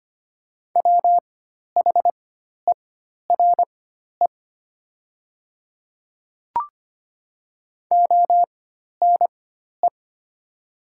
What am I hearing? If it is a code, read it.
Morse code, heard as a keyed tone, about 25 wpm, spaced as for 12 wpm, 700 Hz. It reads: WHERE ONE
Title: 4x Character Spacing - Top 200 Words - 25wpm